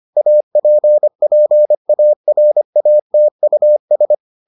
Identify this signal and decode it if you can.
Morse code, heard as a keyed tone, about 25 wpm, 600 Hz. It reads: APPARATUS